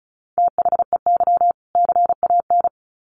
Morse code, heard as a keyed tone, about 34 wpm, 700 Hz. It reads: THEY CAN
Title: Sets of 2 Words 35wpm